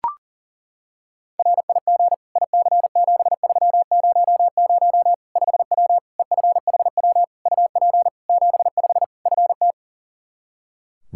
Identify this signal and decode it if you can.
Morse code, heard as a keyed tone, about 37 wpm, 700 Hz. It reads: RIG IC7300 5W EFHW UP 75 FT
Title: Practice QSO - Exchange 2 - Info - 40wpm